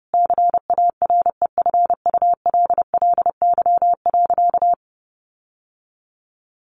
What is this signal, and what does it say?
Morse code, heard as a keyed tone, about 30 wpm, 700 Hz. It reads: CAREFULLY.
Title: Sentences from Top 1300 Words 30wpm